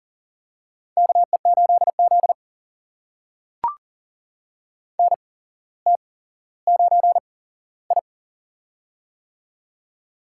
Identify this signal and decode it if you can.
Morse code, heard as a keyed tone, about 38 wpm, spaced as for 17 wpm, 700 Hz. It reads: KE8Z NT9I
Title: USA Call Signs - 4 Characters - 8x Character Spacing - 40wpm